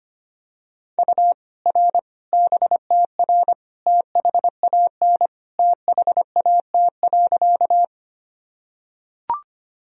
Morse code, heard as a keyed tone, about 25 wpm, 700 Hz. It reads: U R BTR THAN THAT.
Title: Sentences from Top 400 Words and Abbreviations 25wpm